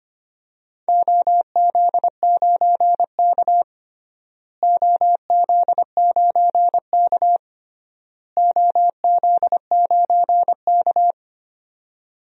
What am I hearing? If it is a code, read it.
Morse code, heard as a keyed tone, about 25 wpm, 700 Hz. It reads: OZ9K OZ9K OZ9K
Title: DXCC-100 Call Signs - 4 Characters - Repeated 3x - 25wpm